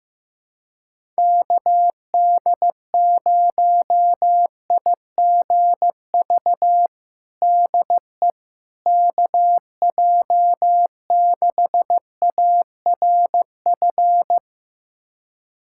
Morse code, heard as a keyed tone, about 15 wpm, 700 Hz. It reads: KD0IGV DE KJ6ARF
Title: USA Callsign DE Callsign - 6 Characters - 15wpm